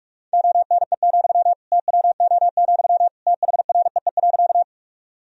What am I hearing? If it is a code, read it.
Morse code, heard as a keyed tone, about 46 wpm, 700 Hz. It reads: ONE, TWO, THREE.
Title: Sentences from Top 1000 Words - Review Part 1 - 45wpm